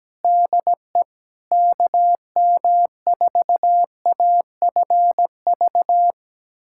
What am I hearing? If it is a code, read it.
Morse code, heard as a keyed tone, about 17 wpm, 700 Hz. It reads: DE KM4AFV
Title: USA Callsign DE Callsign - 6 Characters - 17wpm